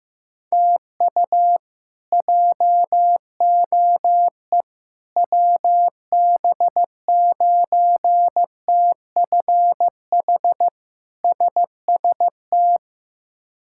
Morse code, heard as a keyed tone, about 15 wpm, 700 Hz. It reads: TU JOE WB9TFH SST
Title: K1USN SST QSO Practice - 15wpm